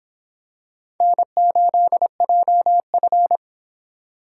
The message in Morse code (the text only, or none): N8JF